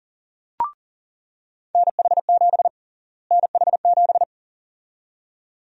Morse code, heard as a keyed tone, about 40 wpm, 700 Hz. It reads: NH7 NH7
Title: USA Call Sign Prefixes - Repeated 2x - 40wpm